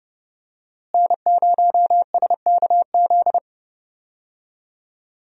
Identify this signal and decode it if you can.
Morse code, heard as a keyed tone, about 32 wpm, 700 Hz. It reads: N0SKZ
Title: USA Call Signs - 5 Characters - Repeated 3x - 30wpm